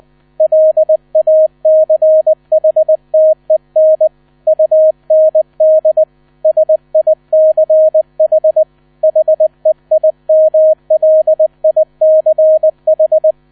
Morse code, heard as a keyed tone, about 20 wpm, 630 Hz. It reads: LACHTEN UND SICH HEIMLICH
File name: 000 Vorwort 20wpm